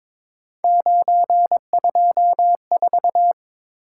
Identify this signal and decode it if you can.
Morse code, heard as a keyed tone, about 23 wpm, 700 Hz. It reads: 924
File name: practice / 22wpm / Three Letter-Number Combinations - 22wpm